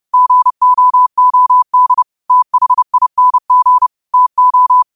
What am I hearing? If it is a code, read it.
Morse code, heard as a keyed tone, about 30 wpm, 1000 Hz. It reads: GOOD THING TO
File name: qotdfiles / 30wpm